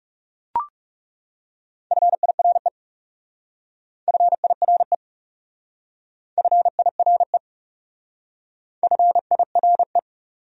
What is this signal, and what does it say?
Morse code, heard as a keyed tone, about 35 wpm, 700 Hz. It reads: FIRE FIRE FIRE FIRE
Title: Speed-Racing Top 500 Words 30wpm